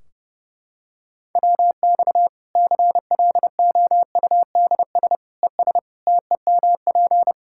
Morse code, heard as a keyed tone, about 31 wpm, 700 Hz. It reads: WX CLOUDS ES TEMP
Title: K1USN SST QSO Practice 30wpm